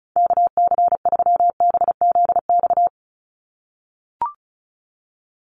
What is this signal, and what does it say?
Morse code, heard as a keyed tone, about 34 wpm, 700 Hz. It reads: KC3BZX
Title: USA Call Signs 35wpm